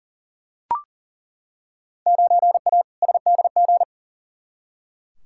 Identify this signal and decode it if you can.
Morse code, heard as a keyed tone, about 38 wpm, 700 Hz. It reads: 9A SDG
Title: Field Day Exchanges Encore - 40wpm